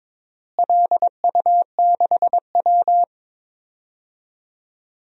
Morse code, heard as a keyed tone, about 22 wpm, 700 Hz. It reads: LU6W
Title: DXCC-100 Call Signs - 4 Characters - Repeated 3x - 22wpm